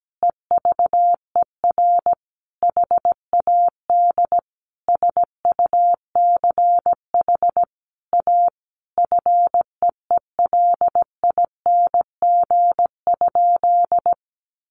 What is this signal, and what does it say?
Morse code, heard as a keyed tone, about 17 wpm, 700 Hz. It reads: EVER HAD SUCH A FEELING?
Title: Sentences from Top 1200 Words 17wpm